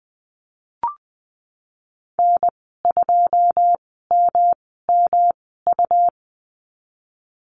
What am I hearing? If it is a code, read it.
Morse code, heard as a keyed tone, about 20 wpm, spaced as for 15 wpm, 700 Hz. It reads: N2MMU
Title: 147 Lesson 24g - Letter y - Call Signs 20wpm